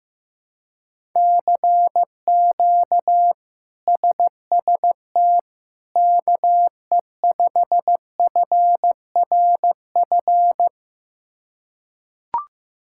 Morse code, heard as a keyed tone, about 15 wpm, 700 Hz. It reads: CQ SST KE5FRF
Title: K1USN SST QSO Practice - 15wpm